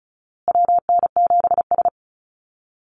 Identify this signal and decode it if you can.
Morse code, heard as a keyed tone, about 34 wpm, 700 Hz. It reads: WN7S
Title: Call Signs Repeated 2x - 35wpm